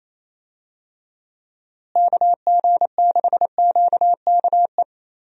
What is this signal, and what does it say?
Morse code, heard as a keyed tone, about 28 wpm, 700 Hz. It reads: KG6QKE
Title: USA Call Signs - 6 Characters - Repeated 2x - 28wpm